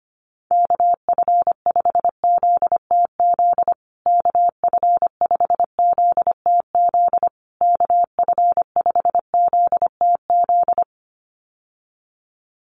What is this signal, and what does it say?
Morse code, heard as a keyed tone, about 25 wpm, 700 Hz. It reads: KF5ZTZ KF5ZTZ KF5ZTZ
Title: USA Call Signs Repeated 3x 25wpm